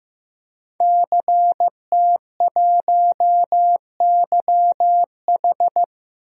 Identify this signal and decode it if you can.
Morse code, heard as a keyed tone, about 15 wpm, 700 Hz. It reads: CT1YH
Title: DXCC-100 Call Signs - 5 Characters - Repeated 2x - 15wpm